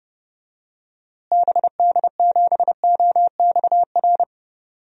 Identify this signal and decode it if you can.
Morse code, heard as a keyed tone, about 30 wpm, 700 Hz. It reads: BD7OXR
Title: DXCC-100 Call Signs - 6 Characters - Repeated 2x - 30wpm